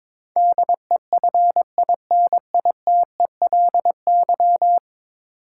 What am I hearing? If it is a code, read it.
Morse code, heard as a keyed tone, about 22 wpm, 700 Hz. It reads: DEFINITELY
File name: practice / 22wpm / 10 Letter Words - First 3 Letters Given - 22wpm